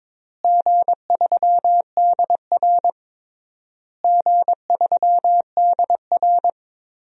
Morse code, heard as a keyed tone, about 22 wpm, 700 Hz. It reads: G3DR G3DR
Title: DXCC-100 Call Signs - 4 Characters - Repeated 3x - 22wpm